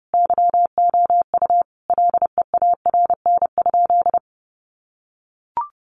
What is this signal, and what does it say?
Morse code, heard as a keyed tone, about 30 wpm, 700 Hz. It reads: YOU LEARN?
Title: Sentences from Top 1200 Words 30wpm